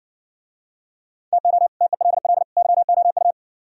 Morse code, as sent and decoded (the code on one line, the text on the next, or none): - --- / - . .-.. .-.. / -.-- --- ..-
TO TELL YOU